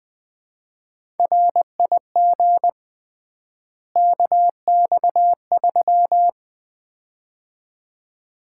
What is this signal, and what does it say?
Morse code, heard as a keyed tone, about 20 wpm, 700 Hz. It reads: RIG KX3
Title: Common QSO Phrases - 3x Word-Spacing - 20wpm